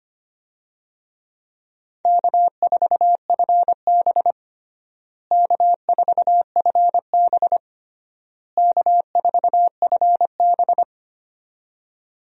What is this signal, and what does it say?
Morse code, heard as a keyed tone, about 25 wpm, 700 Hz. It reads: K4FB K4FB K4FB
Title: USA Call Signs - 4 Characters - Repeated 3x - 25wpm